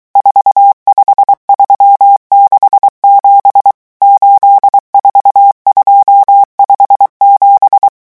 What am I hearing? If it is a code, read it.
Morse code, heard as a keyed tone, about 23 wpm, 800 Hz. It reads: 45367 84257
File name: morse mp3 / revision11 chiffres vitesse 22